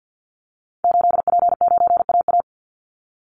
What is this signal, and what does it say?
Morse code, heard as a keyed tone, about 50 wpm, 700 Hz. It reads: 8P9AA
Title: Call Signs - 50wpm